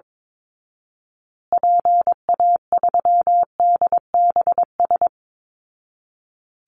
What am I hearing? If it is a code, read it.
Morse code, heard as a keyed tone, about 22 wpm, 700 Hz. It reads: PA3DBS
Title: Call Signs Repeated 3x - 22wpm